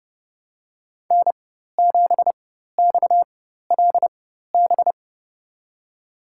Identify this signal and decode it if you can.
Morse code, heard as a keyed tone, about 30 wpm, spaced as for 14 wpm, 700 Hz. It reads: N7XLB
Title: USA Call Signs - 5 Characters - 4x Character Spacing - 30wpm